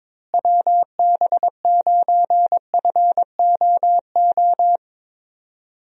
Morse code, heard as a keyed tone, about 22 wpm, 700 Hz. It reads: WB9FOO
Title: USA Call Signs - 6 Characters - Repeated 3x - 22wpm